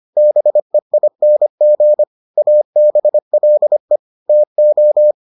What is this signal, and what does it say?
Morse code, heard as a keyed tone, about 25 wpm, 600 Hz. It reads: BEING ABLE TO